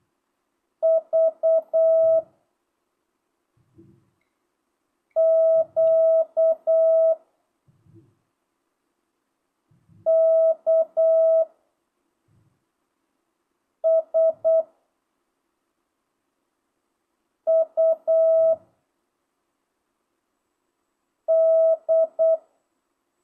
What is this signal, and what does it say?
Morse code, heard as a keyed tone, about 8 wpm, 660 Hz. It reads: V Q K S U D
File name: jh8cbh / 2025cw